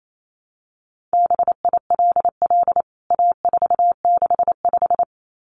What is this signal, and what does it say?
Morse code, heard as a keyed tone, about 28 wpm, 700 Hz. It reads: BILL A465
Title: CWT Exchange 28wpm